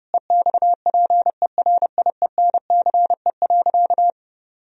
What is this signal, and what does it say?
Morse code, heard as a keyed tone, about 30 wpm, 700 Hz. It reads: EXPERIENCE.
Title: Sentences from Top 1000 Words - Review Part 3 - 30wpm